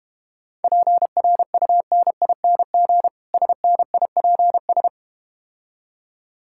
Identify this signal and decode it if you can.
Morse code, heard as a keyed tone, about 32 wpm, 700 Hz. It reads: PRUNING SNIPS